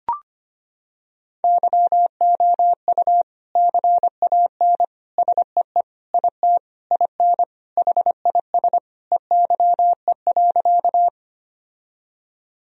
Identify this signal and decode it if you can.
Morse code, heard as a keyed tone, about 25 wpm, 700 Hz. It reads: YOU CAN SEE IT IN HIS EYE.
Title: Sentences from Top 1000 Words - Review Part 1 - 25wpm